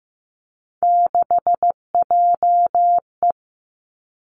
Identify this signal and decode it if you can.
Morse code, heard as a keyed tone, about 15 wpm, 700 Hz. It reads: 6JE
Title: Three Challenging Letter-Number Combinations 15wpm